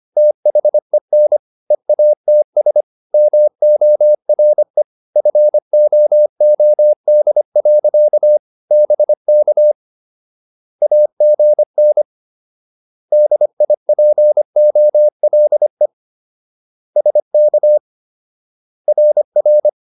Morse code, heard as a keyed tone, about 25 wpm, 600 Hz. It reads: THEN EATS MORE FOOD. BK AGN DIPOLE SK RR